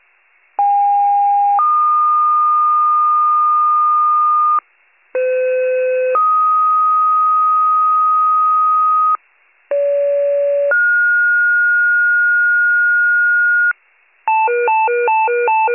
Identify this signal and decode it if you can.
Two-tone page, heard as a one-second tone, then a three-second tone, about 810 then 1220 Hz, about 510 then 1250 Hz, about 570 then 1430 Hz. It is heard from 0.6 to 13.7 s.